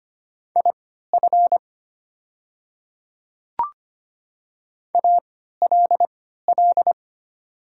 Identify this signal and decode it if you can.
Morse code, heard as a keyed tone, about 25 wpm, spaced as for 14 wpm, 700 Hz. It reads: IF ALL
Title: 3x Character Spacing - Top 100 Words - 25wpm